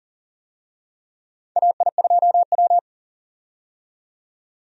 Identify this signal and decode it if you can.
Morse code, heard as a keyed tone, about 38 wpm, 700 Hz. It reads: AI2W
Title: USA Call Signs - 4 Characters - 40wpm